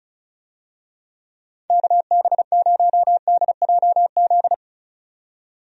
Morse code, heard as a keyed tone, about 36 wpm, 700 Hz. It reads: KB0DJZ
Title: USA Call Signs - 6 Characters - 35wpm